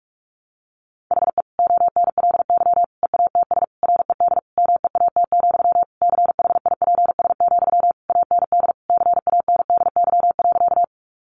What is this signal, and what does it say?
Morse code, heard as a keyed tone, about 44 wpm, 700 Hz. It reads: HE ONLY EATS RED MEAT, CHIPS, AND CANDY.
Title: Easy Phrases 45wpm